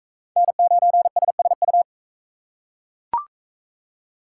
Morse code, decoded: N9SSU

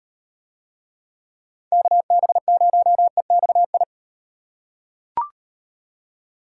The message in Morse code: KB0EXI

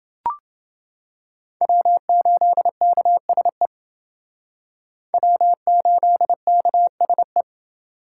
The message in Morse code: W8KSE W8KSE

